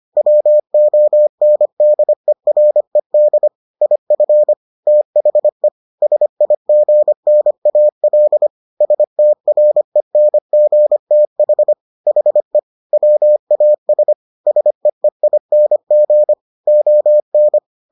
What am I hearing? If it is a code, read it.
Morse code, heard as a keyed tone, about 25 wpm, 600 Hz. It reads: WONDERED IF THE SIGNAL STRENGTH HE WAS SEEING ON